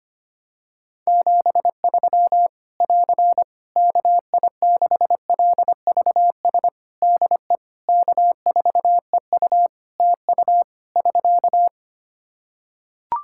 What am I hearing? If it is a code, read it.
Morse code, heard as a keyed tone, about 25 wpm, 700 Hz. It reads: 73 <AR> KI6LVS DE K4EU TU <SK>
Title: K1USN SST QSO Practice 25wpm